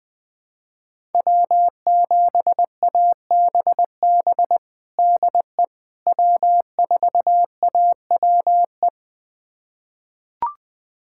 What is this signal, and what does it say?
Morse code, heard as a keyed tone, about 20 wpm, 700 Hz. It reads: W7ABB DE W4AWE
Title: USA Callsign DE Callsign - 5 Characters - 20wpm